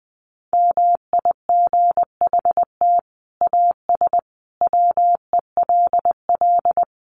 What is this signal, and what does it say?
Morse code, heard as a keyed tone, about 20 wpm, 700 Hz. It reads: MIGHT AS WELL
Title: Sets of 5 Words Encore2 20wpm